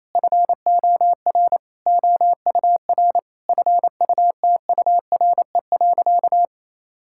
Morse code, heard as a keyed tone, about 28 wpm, 700 Hz. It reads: FOR OUR FUTURE.
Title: Sentences from Top 1000 Words - Review Part 6 - 28wpm